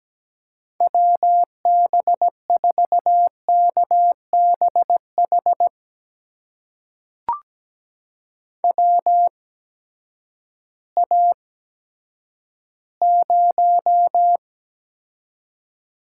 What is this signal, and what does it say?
Morse code, heard as a keyed tone, about 17 wpm, spaced as for 10 wpm, 700 Hz. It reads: WB4KBH WA0
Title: USA Call Signs - 6 Characters - 8x Character Spacing - 17wpm